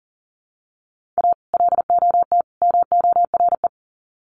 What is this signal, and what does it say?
Morse code, heard as a keyed tone, about 40 wpm, 700 Hz. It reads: A LOT MORE
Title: Sets of 3 Words 40wpm